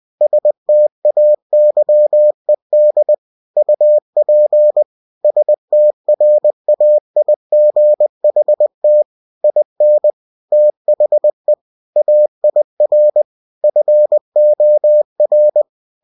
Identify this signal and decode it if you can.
Morse code, heard as a keyed tone, about 20 wpm, 600 Hz. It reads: STAYED UP STRAIGHT IN THE AIR FOR